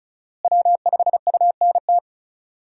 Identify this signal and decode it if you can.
Morse code, heard as a keyed tone, about 36 wpm, 700 Hz. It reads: W5UNT
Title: USA Call Signs - 5 Characters - Repeated 3x - 35wpm